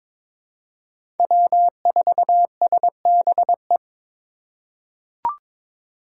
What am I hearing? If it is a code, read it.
Morse code, heard as a keyed tone, about 22 wpm, 700 Hz. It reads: W4SBE
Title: USA Call Signs - 5 Characters - 4x Character Spacing - 22wpm